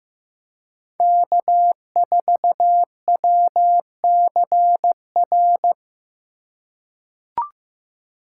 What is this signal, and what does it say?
Morse code, heard as a keyed tone, about 15 wpm, 700 Hz. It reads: K4WCR
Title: USA Call Signs - 5 Characters - 3x Character Spacing - 15wpm